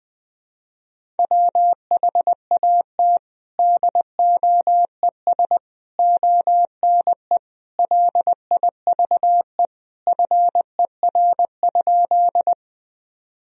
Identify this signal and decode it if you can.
Morse code, heard as a keyed tone, about 20 wpm, 700 Hz. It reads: WHAT DOES ONE LIVE FER?
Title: Sentences from Top 200 Words and Abbreviations 20wpm